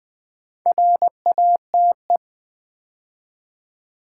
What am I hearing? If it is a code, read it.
Morse code, heard as a keyed tone, about 20 wpm, 700 Hz. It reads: RATE
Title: Build A Word - 2 Letters At A Time - 4-Letter Words - 20wpm